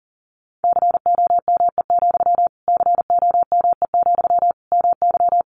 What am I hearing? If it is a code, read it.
Morse code, heard as a keyed tone, about 40 wpm, 700 Hz. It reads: COME, COME, MY
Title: Sentences from Top 1500 Words 40wpm